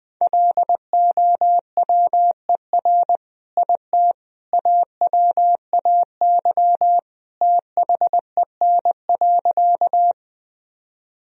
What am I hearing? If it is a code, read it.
Morse code, heard as a keyed tone, about 20 wpm, 700 Hz. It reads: LOWER IT AWAY THEN.
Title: Sentences from Top 1000 Words - Review Part 4 - 20wpm